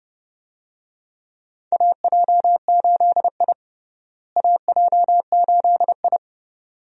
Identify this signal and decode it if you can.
Morse code, heard as a keyed tone, about 30 wpm, 700 Hz. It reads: AJ8I AJ8I
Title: USA Call Signs - 4 Characters - Repeated 2x - 30wpm